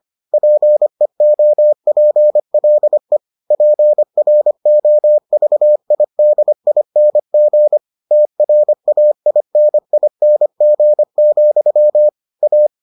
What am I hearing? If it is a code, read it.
Morse code, heard as a keyed tone, about 25 wpm, 600 Hz. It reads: PEOPLE PROVIDING TRAINING, A